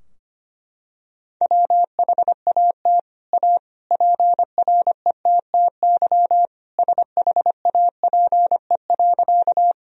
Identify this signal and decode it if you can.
Morse code, heard as a keyed tone, about 25 wpm, 700 Hz. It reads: WHAT A PRETTY SHAPE.
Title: Sentences from Top 1000 Words - Review Part 3 - 25wpm